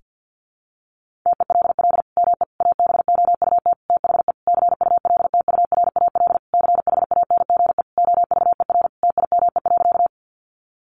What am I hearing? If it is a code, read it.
Morse code, heard as a keyed tone, about 50 wpm, 700 Hz. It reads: TELL ME ABOUT THE CULTURAL CHANGE OVER TIME.
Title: Sentences from Top 1500 Words 50wpm